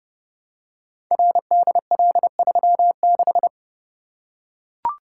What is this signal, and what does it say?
Morse code, heard as a keyed tone, about 30 wpm, 700 Hz. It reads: RDL36
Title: Code Groups - 5 Letter:Numbers - 30wpm